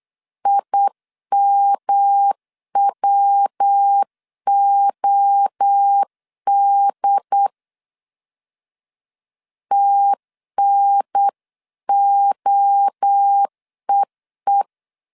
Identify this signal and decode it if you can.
Morse code, heard as a keyed tone, about 8 wpm, 800 Hz. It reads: IMWOD TNOEE